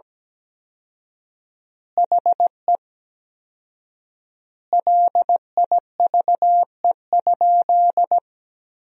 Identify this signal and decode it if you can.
Morse code, heard as a keyed tone, about 17 wpm, 700 Hz. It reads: HE LIVE?
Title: 4x Word-Spacing - Sentences from Top 200 Words - 17wpm